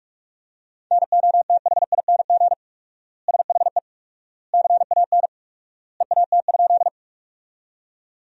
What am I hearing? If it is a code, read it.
Morse code, heard as a keyed tone, about 42 wpm, 700 Hz. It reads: NOTHING SHE CAN EAT?
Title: 4x Word-Spacing - Sentences from Top 500 Words - 45wpm